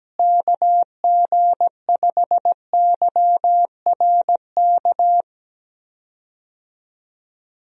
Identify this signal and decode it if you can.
Morse code, heard as a keyed tone, about 17 wpm, 700 Hz. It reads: KG5YRK
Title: USA Call Signs - 6 Characters - Repeated 3x - 17wpm